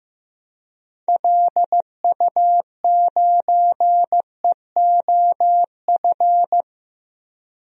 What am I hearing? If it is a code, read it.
Morse code, heard as a keyed tone, about 15 wpm, 700 Hz. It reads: LU9EOF
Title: DXCC-100 Call Signs - 6 Characters - 15wpm